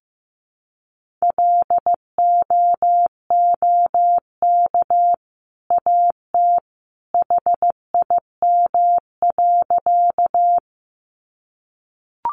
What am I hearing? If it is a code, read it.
Morse code, heard as a keyed tone, about 15 wpm, 700 Hz. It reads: LOOK AT HIM.
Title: Sentences from Top 100 Words 15wpm